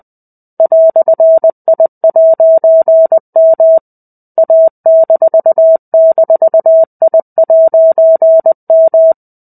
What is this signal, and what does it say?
Morse code, heard as a keyed tone, about 20 wpm, 650 Hz. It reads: "I'M A--I'M
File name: 05-Chapter-05 20 WPM